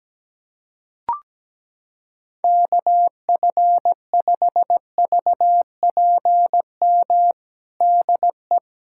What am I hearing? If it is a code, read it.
Morse code, heard as a keyed tone, about 17 wpm, 700 Hz. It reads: KF5VPM DE
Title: USA Callsign DE Callsign - 6 Characters - 17wpm